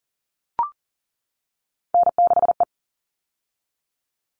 Morse code, heard as a keyed tone, about 40 wpm, 700 Hz. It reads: N6E
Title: All Cut Numbers - 3-Digit 40wpm